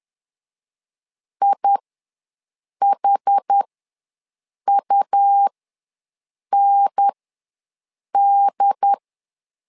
Morse code, heard as a keyed tone, about 10 wpm, spaced as for 6 wpm, 800 Hz. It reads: IHUND
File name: LEZIONI / 4 Q1 ws 10wpm 2538